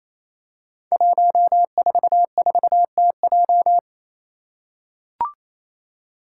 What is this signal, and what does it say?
Morse code, heard as a keyed tone, about 27 wpm, 700 Hz. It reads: 144TJ
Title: Five Challenging Letter-Number Combinations 28wpm